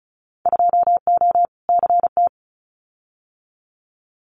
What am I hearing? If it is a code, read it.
Morse code, heard as a keyed tone, about 36 wpm, 700 Hz. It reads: 2O CT
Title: Winter Field Day Exchanges 35wpm